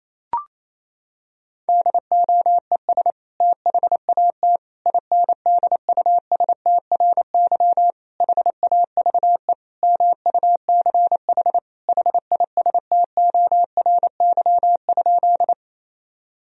Morse code, heard as a keyed tone, about 28 wpm, 700 Hz. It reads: DOES THAT INDUSTRY HAVE MUCH HISTORY?
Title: Sentences from Top 1000 Words - Review Part 4 - 28wpm